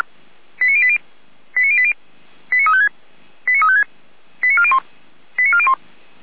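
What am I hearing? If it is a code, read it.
Five-tone selective call (ZVEI), heard as quick runs of five tones, tones of about 70 ms. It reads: 89080 89080 89356 89356 89401 89401
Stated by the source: ZVEI five-tone selective call received by SDR.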